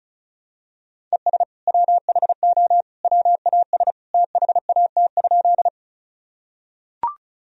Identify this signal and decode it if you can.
Morse code, heard as a keyed tone, about 36 wpm, 700 Hz. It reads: ES WHO WAS THAT?
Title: Sentences from Top 100 Words and Abbreviations 35wpm